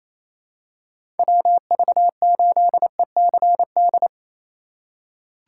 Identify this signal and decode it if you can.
Morse code, heard as a keyed tone, about 28 wpm, 700 Hz. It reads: WV8ECD
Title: USA Call Signs - 2x Character Spacing - 28wpm